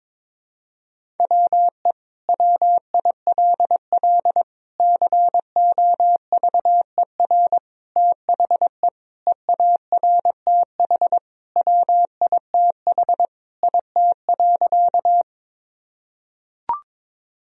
Morse code, heard as a keyed tone, about 21 wpm, 700 Hz. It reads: WE WILL COVER THE EARTH WITH IT.
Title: Sentences from Top 1000 Words - Review Part 1 - 22wpm